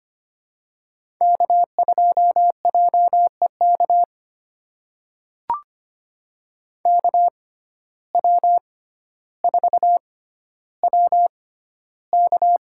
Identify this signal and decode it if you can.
Morse code, heard as a keyed tone, about 25 wpm, spaced as for 13 wpm, 700 Hz. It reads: K2JEK KW4WK
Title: USA Call Signs - 6x Character Spacing - 25wpm